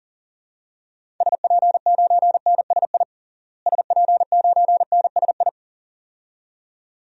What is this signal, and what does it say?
Morse code, heard as a keyed tone, about 40 wpm, 700 Hz. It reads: SP9NSI SP9NSI
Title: DXCC-100 Call Signs - 6 Characters - Repeated 2x - 40wpm